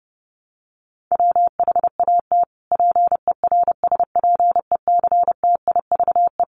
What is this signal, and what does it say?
Morse code, heard as a keyed tone, about 30 wpm, 700 Hz. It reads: WHAT PERSPECTIVE
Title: Sentences from Top 1200 Words 30wpm